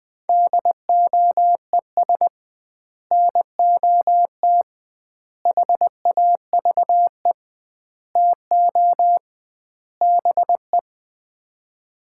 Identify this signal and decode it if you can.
Morse code, heard as a keyed tone, about 21 wpm, 700 Hz. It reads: DOES NOT HAVE TO BE
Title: 2x Word-Spacing - Sets of 5 Words - 20wpm